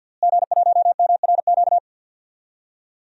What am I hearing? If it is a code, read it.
Morse code, heard as a keyed tone, about 52 wpm, 700 Hz. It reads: G1MRQ
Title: DXCC-100 Call Signs - 5 Characters - Repeated 2x - 50wpm